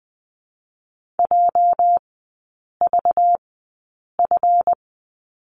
Morse code, heard as a keyed tone, about 20 wpm, 700 Hz. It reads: J V F
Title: 212 Lesson 34e - Number 8 - Three Characters 20wpm